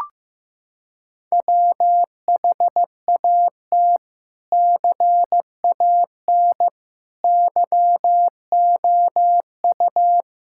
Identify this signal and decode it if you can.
Morse code, heard as a keyed tone, about 15 wpm, 700 Hz. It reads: WHAT CAN YOU
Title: Sentences from Top 1600 Words 15wpm